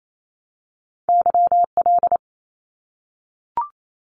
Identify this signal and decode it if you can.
Morse code, heard as a keyed tone, about 28 wpm, 700 Hz. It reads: YL
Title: Abbreviations Repeated 3x - 28wpm